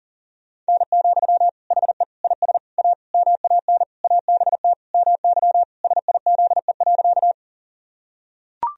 Morse code, heard as a keyed tone, about 38 wpm, 700 Hz. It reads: N, HE IS A MAN ABT MY SIZE.
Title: Sentences from Top 400 Words and Abbreviations 40wpm